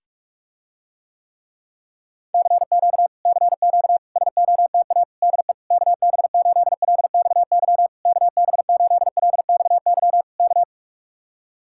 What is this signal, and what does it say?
Morse code, heard as a keyed tone, about 50 wpm, 700 Hz. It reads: CQ CQ SOTA DE KB8LXY KB8LXY K